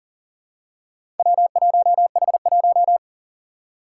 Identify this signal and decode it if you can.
Morse code, heard as a keyed tone, about 40 wpm, 700 Hz. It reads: W1H1
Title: Four Challenging Letter-Number Combinations 40wpm